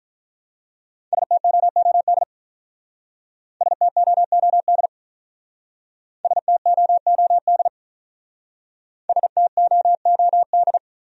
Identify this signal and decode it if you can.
Morse code, heard as a keyed tone, about 45 wpm, 700 Hz. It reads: STOOD STOOD STOOD STOOD